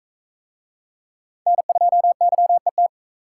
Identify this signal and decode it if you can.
Morse code, heard as a keyed tone, about 39 wpm, 700 Hz. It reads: N2YET